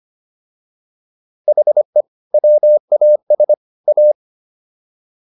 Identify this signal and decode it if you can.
Morse code, heard as a keyed tone, about 25 wpm, 600 Hz. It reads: HE WAS A